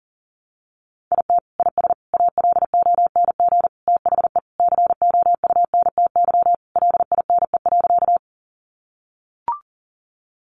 Morse code, heard as a keyed tone, about 40 wpm, 700 Hz. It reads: IT IS ALONG THE COUNTY LINE.
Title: Sentences from Top 1200 Words 40wpm